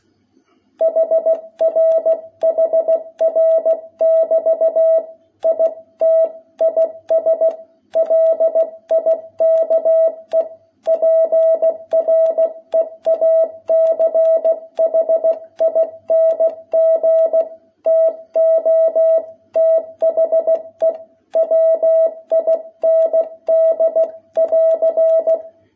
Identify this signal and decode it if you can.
Morse code, heard as a keyed tone, about 16 wpm, 660 Hz. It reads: HR HR<BT> IT IS LIKE PREACHING TO THE WIND<AR>
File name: jh8cbh / 2021cw